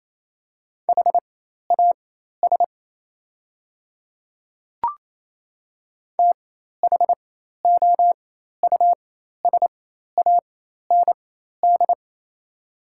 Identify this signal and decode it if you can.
Morse code, heard as a keyed tone, about 28 wpm, spaced as for 13 wpm, 690 Hz. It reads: HAS THOUSAND
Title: 4x Character Spacing - Top 500 Words - 28wpm